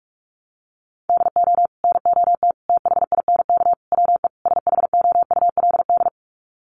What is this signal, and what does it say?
Morse code, heard as a keyed tone, about 44 wpm, 700 Hz. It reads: DO NOT THINK WE SHOULD
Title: Sets of 5 Words Encore 45wpm